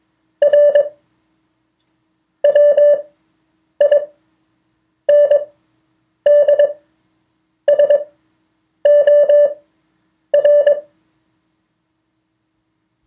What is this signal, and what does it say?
Morse code, heard as a keyed tone, about 21 wpm, spaced as for 8 wpm, 580 Hz. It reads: R WINDSOR